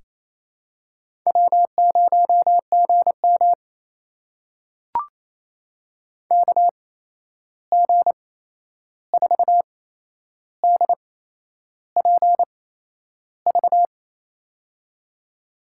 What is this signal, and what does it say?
Morse code, heard as a keyed tone, about 28 wpm, spaced as for 11 wpm, 700 Hz. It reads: W0GM KG4DPV
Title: USA Call Signs - 8x Character Spacing - 28wpm